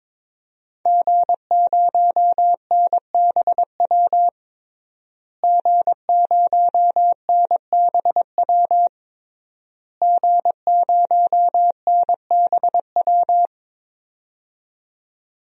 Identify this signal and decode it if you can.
Morse code, heard as a keyed tone, about 22 wpm, 700 Hz. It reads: G0NBW G0NBW G0NBW
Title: DXCC-100 Call Signs - 5 Characters - Repeated 3x - 22wpm